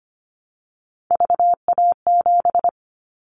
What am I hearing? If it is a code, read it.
Morse code, heard as a keyed tone, about 25 wpm, 700 Hz. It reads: VA7
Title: Callsign Prefixes for Top 25 DXCC - Repeated 3x - 25wpm